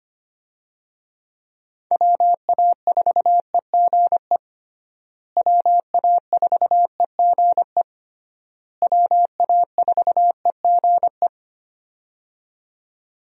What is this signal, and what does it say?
Morse code, heard as a keyed tone, about 25 wpm, 700 Hz. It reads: WA4EGE WA4EGE WA4EGE
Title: USA Call Signs - 6 Characters - Repeated 3x - 25wpm